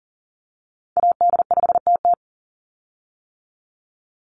Morse code, heard as a keyed tone, about 40 wpm, 700 Hz. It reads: AD5TT
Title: Call Signs - 40wpm